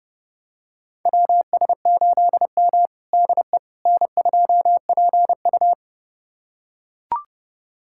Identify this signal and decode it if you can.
Morse code, heard as a keyed tone, about 30 wpm, 700 Hz. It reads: WS8M DE N2PU
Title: USA Callsign DE Callsign - 4 Characters - 30wpm